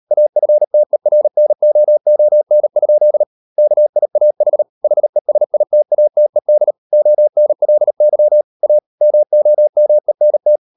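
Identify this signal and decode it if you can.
Morse code, heard as a keyed tone, about 38 wpm, 600 Hz. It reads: AFTERNOON? KIAH HESITATED ONLY A MOMENT
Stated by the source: Morse code practice at 38 WPM, sending Ken Follett's novel Never in English, from CW med Gnister.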